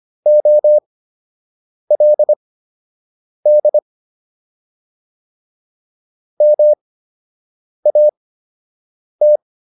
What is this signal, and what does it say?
Morse code, heard as a keyed tone, about 25 wpm, spaced as for 7 wpm, 600 Hz. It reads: OLD MAT